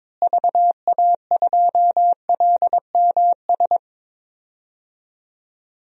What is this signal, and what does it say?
Morse code, heard as a keyed tone, about 22 wpm, 700 Hz. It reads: VA2LMS
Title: DXCC-100 Call Signs - 6 Characters - Repeated 2x - 22wpm